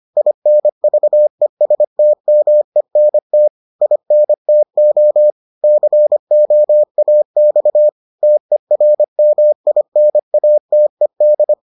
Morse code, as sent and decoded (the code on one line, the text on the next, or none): .. -. ...- . ... - -- . -. - / .. -. - --- / -.-. --- .- -..- / - . .-. -- .. -. .- - . -..
INVESTMENT INTO COAX TERMINATED